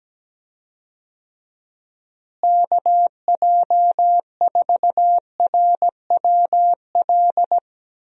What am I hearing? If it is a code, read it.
Morse code, heard as a keyed tone, about 17 wpm, 700 Hz. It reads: KJ4RWL
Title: USA Call Signs - 6 Characters - Repeated 3x - 17wpm